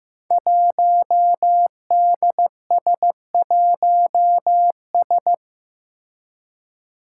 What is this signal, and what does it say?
Morse code, heard as a keyed tone, about 15 wpm, 700 Hz. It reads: 1DS1S
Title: Five Challenging Letter-Number Combinations 15wpm